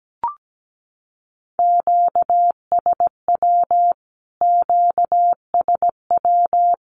Morse code, heard as a keyed tone, about 17 wpm, 700 Hz. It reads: QSW QSW
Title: Q-Codes Repeated 3x 17wpm